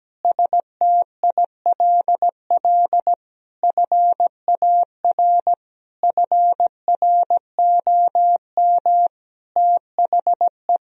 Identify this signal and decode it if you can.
Morse code, heard as a keyed tone, about 17 wpm, 700 Hz. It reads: STILL FAR FROM THE
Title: Sentences from Top 1000 Words - Review Part 4 - 17wpm